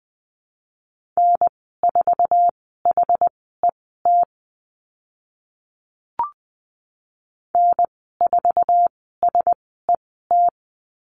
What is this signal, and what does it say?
Morse code, heard as a keyed tone, about 20 wpm, spaced as for 14 wpm, 700 Hz. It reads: N4HET N4SET
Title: 076 Lesson 13g - Letter L - Call Signs 20wpm